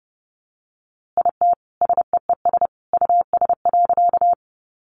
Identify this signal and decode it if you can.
Morse code, heard as a keyed tone, about 30 wpm, 700 Hz. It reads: IT SEES US.
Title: A Sound Of Thunder 30wpm